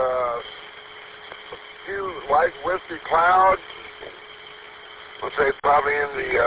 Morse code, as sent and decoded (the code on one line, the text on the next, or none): none